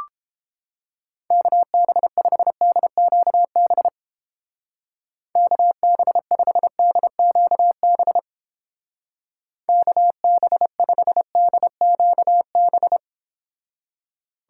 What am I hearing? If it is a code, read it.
Morse code, heard as a keyed tone, about 31 wpm, 700 Hz. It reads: KB5DQB KB5DQB KB5DQB